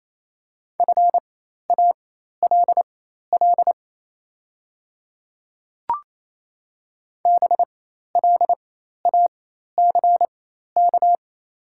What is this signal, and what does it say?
Morse code, heard as a keyed tone, about 27 wpm, spaced as for 13 wpm, 690 Hz. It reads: FALL BLACK
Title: 4x Character Spacing - Top 500 Words - 28wpm